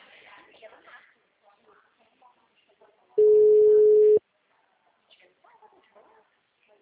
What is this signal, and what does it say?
Ringback tone (Europe, 425 Hz), with speech.
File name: out- -2013- - - .247